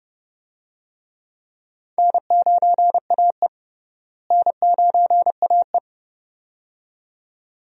Morse code, heard as a keyed tone, about 32 wpm, 700 Hz. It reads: N9AE N9AE
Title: USA Call Signs - 4 Characters - Repeated 2x - 30wpm